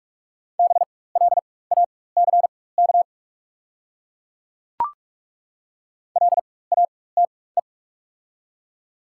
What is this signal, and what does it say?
Morse code, heard as a keyed tone, about 48 wpm, spaced as for 21 wpm, 690 Hz. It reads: BLACK LATE
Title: 4x Character Spacing - Top 500 Words - 45wpm